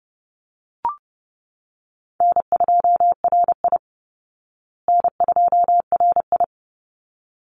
Morse code, heard as a keyed tone, about 30 wpm, 700 Hz. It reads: N2RI N2RI
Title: Call Signs Repeated 3x - 30wpm